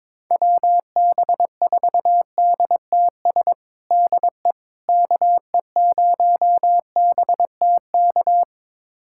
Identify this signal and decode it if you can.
Morse code, heard as a keyed tone, about 21 wpm, 700 Hz. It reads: WB4DTS DE KE0BTK
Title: USA Callsign DE Callsign - 6 Characters - 22wpm